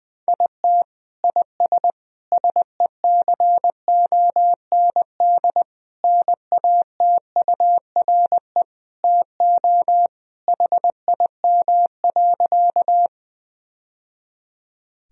Morse code, heard as a keyed tone, about 21 wpm, 700 Hz. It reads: IT IS SECOND NATURE TO HIM.
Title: Sentences from Top 1000 Words - Review Part 1 - 20wpm